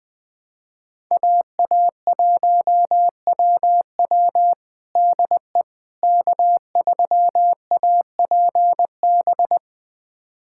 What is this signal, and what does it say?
Morse code, heard as a keyed tone, about 20 wpm, 700 Hz. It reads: AA1WW DE K3APB
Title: USA Callsign DE Callsign - 5 Characters - 20wpm